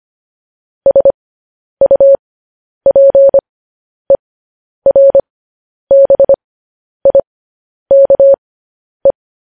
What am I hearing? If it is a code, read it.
Morse code, heard as a keyed tone, about 25 wpm, spaced as for 10 wpm, 550 Hz. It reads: SUPERBIKE
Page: news-headlines-in-morse-code-at-10-wpm